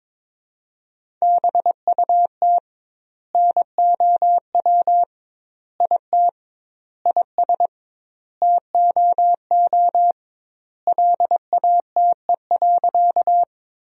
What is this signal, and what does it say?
Morse code, heard as a keyed tone, about 22 wpm, 700 Hz. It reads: BUT NOW IT IS TOO LATE.
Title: 2x Word-Spacing - Sentences from Top 300 Words - 22wpm